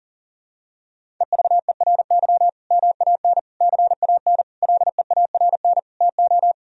Morse code, heard as a keyed tone, about 40 wpm, 700 Hz. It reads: EVERY MAN CAN LEARN TO
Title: Sentences from Top 1000 Words - Review Part 1 - 40wpm